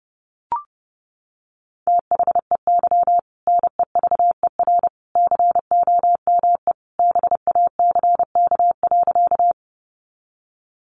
Morse code, heard as a keyed tone, about 30 wpm, 700 Hz. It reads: THEY NEVER COME BACK.
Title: Sentences from Top 300 Words 30wpm